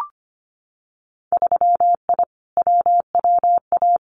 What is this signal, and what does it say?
Morse code, heard as a keyed tone, about 25 wpm, 700 Hz. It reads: 3I WWA
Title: Winter Field Day Exchanges Encore 25wpm